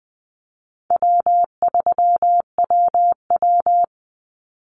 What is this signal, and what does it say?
Morse code, heard as a keyed tone, about 20 wpm, 700 Hz. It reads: W3WW
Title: Call Signs Repeated 2x - 20wpm